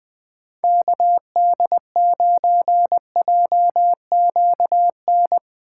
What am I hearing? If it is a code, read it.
Morse code, heard as a keyed tone, about 20 wpm, 700 Hz. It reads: KD9JQN
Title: USA Call Signs - 6 Characters - 4x Character Spacing - 20wpm